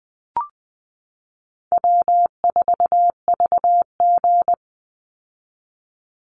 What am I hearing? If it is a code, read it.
Morse code, heard as a keyed tone, about 20 wpm, 700 Hz. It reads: W4VG
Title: Call Signs Repeated 3x - 20wpm